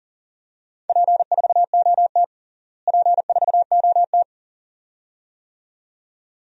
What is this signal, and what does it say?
Morse code, heard as a keyed tone, about 37 wpm, 700 Hz. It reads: P4OT P4OT
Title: DXCC-100 Call Signs - 4 Characters - Repeated 2x - 40wpm